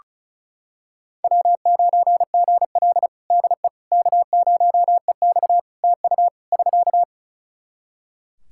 Morse code, heard as a keyed tone, about 34 wpm, 700 Hz. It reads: W9GL DE K0EX TU <SK>
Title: Practice QSO - Exchange 4 - Ending - 35wpm